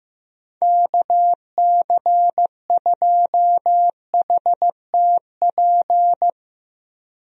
Morse code, heard as a keyed tone, about 15 wpm, 700 Hz. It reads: KC2HTP
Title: USA Call Signs - 6 Characters - 15wpm